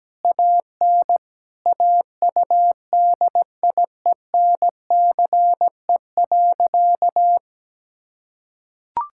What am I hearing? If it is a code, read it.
Morse code, heard as a keyed tone, about 17 wpm, 700 Hz. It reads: AN AUDIENCE.
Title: Sentences from Top 1000 Words - Review Part 5 - 17wpm